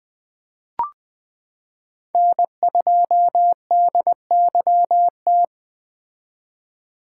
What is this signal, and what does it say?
Morse code, heard as a keyed tone, about 20 wpm, 700 Hz. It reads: N2DYT
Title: K1USN SST QSO Practice - 20wpm